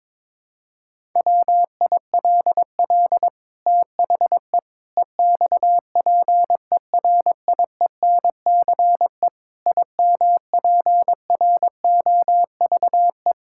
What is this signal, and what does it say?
Morse code, heard as a keyed tone, about 22 wpm, 700 Hz. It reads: WILL THE EXPERIENCE IMPROVE
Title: Sentences from Top 1000 Words - Review Part 5 - 22wpm